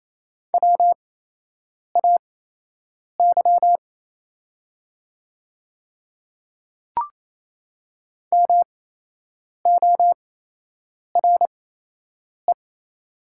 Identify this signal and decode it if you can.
Morse code, heard as a keyed tone, about 28 wpm, spaced as for 8 wpm, 700 Hz. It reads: WAY MORE
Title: 8x Character Spacing - Top 100 Words - 28wpm